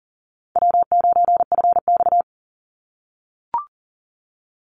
Morse code, heard as a keyed tone, about 39 wpm, 700 Hz. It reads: W9FX
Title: Easy Call Signs - 40wpm